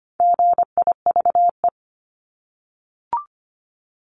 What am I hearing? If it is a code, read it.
Morse code, heard as a keyed tone, about 25 wpm, 700 Hz. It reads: GIVE
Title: Multi-letter Phonemes Repeated 3x 25wpm